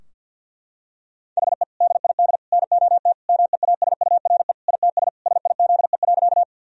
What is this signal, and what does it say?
Morse code, heard as a keyed tone, about 50 wpm, 700 Hz. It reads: HE DID NOT MEASURE ITS SIZE.
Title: Sentences from Top 1000 Words - Review Part 2 - 50wpm